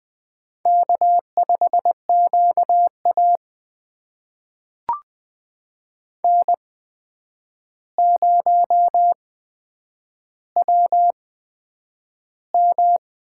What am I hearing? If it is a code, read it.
Morse code, heard as a keyed tone, about 20 wpm, spaced as for 9 wpm, 700 Hz. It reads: K5QA N0WM
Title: USA Call Signs - 8x Character Spacing - 20wpm